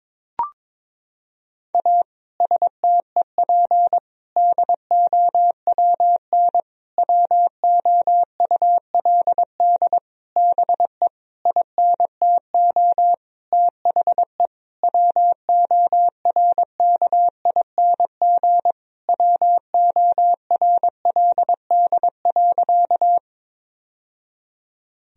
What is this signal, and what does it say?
Morse code, heard as a keyed tone, about 22 wpm, 700 Hz. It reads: A STEP DOWN WOULD BE INTO THE WORKING WORLD.
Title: Sentences from Top 1000 Words - Review Part 4 - 22wpm